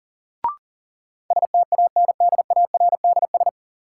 Morse code, heard as a keyed tone, about 37 wpm, 700 Hz. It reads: STANDARDS
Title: Sentences from Top 1000 Words - Review Part 6 - New Word Drill - 40wpm